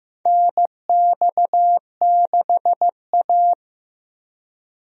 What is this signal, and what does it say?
Morse code, heard as a keyed tone, about 15 wpm, 700 Hz. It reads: NX6A
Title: USA Call Signs - 4 Characters - Repeated 3x - 15wpm